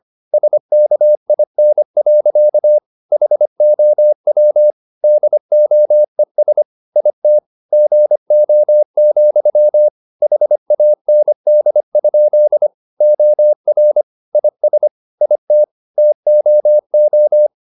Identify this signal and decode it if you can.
Morse code, heard as a keyed tone, about 25 wpm, 600 Hz. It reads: SKIN. HOW DOES IT GO, HAND? OR IS IT TOO
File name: the old man and the sea part 1 25wpm